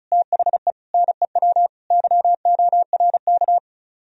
Morse code, heard as a keyed tone, about 44 wpm, 700 Hz. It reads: THE NEW YORK
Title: Speed-Racing - Sets of 4 Words 35wpm